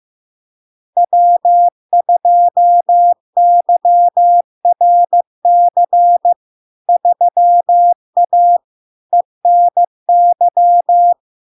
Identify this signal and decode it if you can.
Morse code, heard as a keyed tone, about 15 wpm, 700 Hz. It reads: W2YRC 3A ENY
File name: LARK-FD-2022-Log-in-CW-15wpm-700-Hz